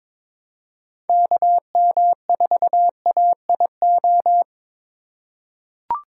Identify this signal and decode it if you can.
Morse code, heard as a keyed tone, about 21 wpm, 700 Hz. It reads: KM4AIO
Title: USA Call Signs - 6 Characters - 2x Character Spacing - 22wpm